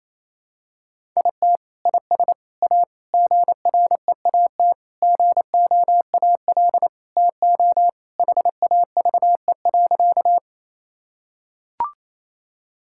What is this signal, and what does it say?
Morse code, heard as a keyed tone, about 28 wpm, 700 Hz. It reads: IT IS A GREAT GOAL TO HAVE.
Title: Sentences from Top 1000 Words - Review Part 5 - 28wpm